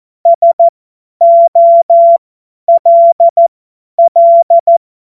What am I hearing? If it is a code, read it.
Morse code, heard as a keyed tone, about 14 wpm, spaced as for 10 wpm, 680 Hz. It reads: SOLL
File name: Morsekurs / Witze / Witz 23. August 2022 50 F70